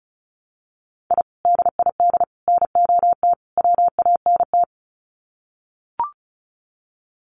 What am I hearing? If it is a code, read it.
Morse code, heard as a keyed tone, about 34 wpm, 700 Hz. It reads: I DID NOT WANT
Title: Sets of 4 Words 35wpm